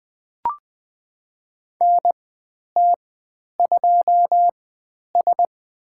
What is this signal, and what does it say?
Morse code, heard as a keyed tone, about 20 wpm, spaced as for 10 wpm, 700 Hz. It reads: NT2S
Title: 109 Lesson 18g - Letter m - Call Signs 20wpm